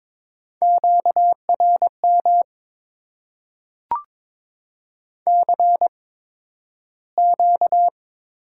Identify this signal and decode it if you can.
Morse code, heard as a keyed tone, about 22 wpm, spaced as for 12 wpm, 700 Hz. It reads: QRM CQ
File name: Top 90 QSO Elements - 8x Character Spacing - 22wpm